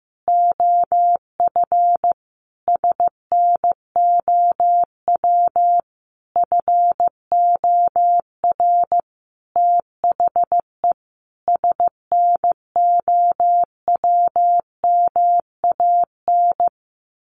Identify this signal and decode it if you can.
Morse code, heard as a keyed tone, about 15 wpm, 700 Hz. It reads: OF SNOW FOR THE SNOWMAN
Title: Easy Phrases 15wpm